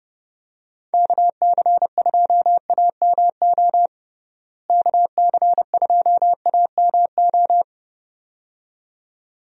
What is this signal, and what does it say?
Morse code, heard as a keyed tone, about 32 wpm, 700 Hz. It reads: KC2AMO KC2AMO
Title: USA Call Signs - 6 Characters - Repeated 2x - 30wpm